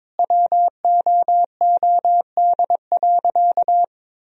Morse code, heard as a keyed tone, about 21 wpm, 700 Hz. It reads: WOOD.
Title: Sentences from Top 1000 Words - Review Part 3 - 22wpm